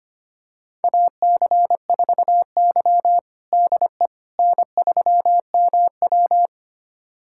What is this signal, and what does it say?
Morse code, heard as a keyed tone, about 25 wpm, 700 Hz. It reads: AC4Y DE N3MW
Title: USA Callsign DE Callsign - 4 Characters - 25wpm